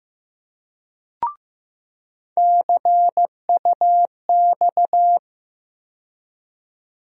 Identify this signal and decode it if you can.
Morse code, heard as a keyed tone, about 15 wpm, 700 Hz. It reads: CUX
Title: USA Call Sign Postfixes - 3 Letters 15wpm